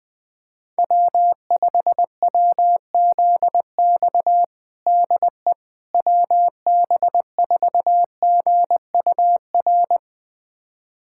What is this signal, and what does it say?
Morse code, heard as a keyed tone, about 20 wpm, 700 Hz. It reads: W5WZX DE WB4GUR
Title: USA Callsign DE Callsign 20wpm